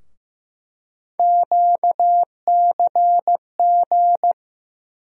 Morse code, heard as a keyed tone, about 15 wpm, 700 Hz. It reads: QCG
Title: USA Call Sign Postfixes - 3 Letters 15wpm